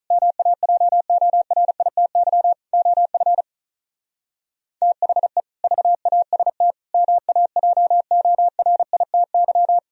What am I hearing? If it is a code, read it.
Morse code, heard as a keyed tone, about 44 wpm, 700 Hz. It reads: MAJORITY OF THE VAST MAJORITY